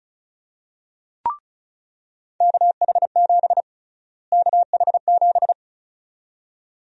Morse code, heard as a keyed tone, about 34 wpm, 700 Hz. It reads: KH7 KH7
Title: USA Call Sign Prefixes - Repeated 2x - 35wpm